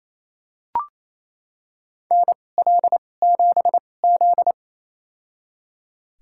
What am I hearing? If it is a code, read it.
Morse code, heard as a keyed tone, about 27 wpm, spaced as for 20 wpm, 700 Hz. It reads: NL7Z
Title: USA Call Signs - 2x Character Spacing - 28wpm